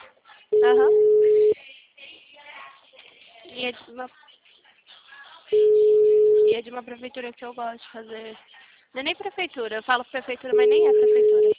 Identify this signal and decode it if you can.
Ringback tone (Europe, 425 Hz), with speech.
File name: out- -2027- - - .133